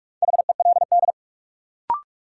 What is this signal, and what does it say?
Morse code, heard as a keyed tone, about 44 wpm, 700 Hz. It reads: HELD
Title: Sentences from Top 1600 Words - New Word Drill - 45wpm